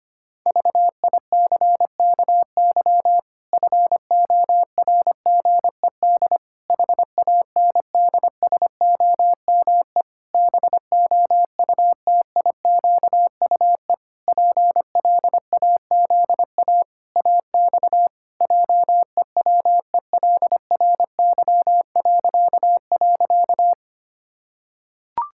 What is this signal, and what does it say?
Morse code, heard as a keyed tone, about 25 wpm, 700 Hz. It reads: VICKY FORGED HANDSOME BOUTIQUE PLAZA AX JEWELRY..
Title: Pangram Challenge v2 - 25wpm